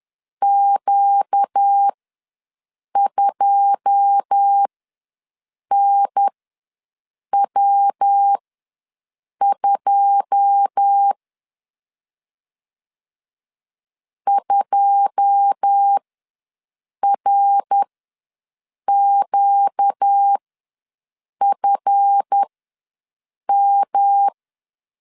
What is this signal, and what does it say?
Morse code, heard as a keyed tone, about 10 wpm, spaced as for 6 wpm, 800 Hz. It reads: Q2NW2 2RQFM
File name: LEZIONI / 9 2 ws 10wpm 4130